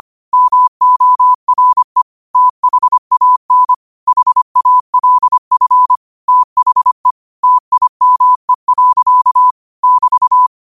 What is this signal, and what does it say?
Morse code, heard as a keyed tone, about 25 wpm, 1000 Hz. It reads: MORE THAN HALF THE TIME. <BT>